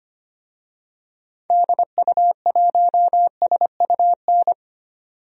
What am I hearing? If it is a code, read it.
Morse code, heard as a keyed tone, about 25 wpm, 700 Hz. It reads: DU1SUN
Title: DXCC-100 Call Signs - 6 Characters - 25wpm